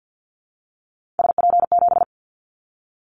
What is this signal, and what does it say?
Morse code, heard as a keyed tone, about 50 wpm, 700 Hz. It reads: SP7
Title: Callsign Prefixes for Top 25 DXCC - 50wpm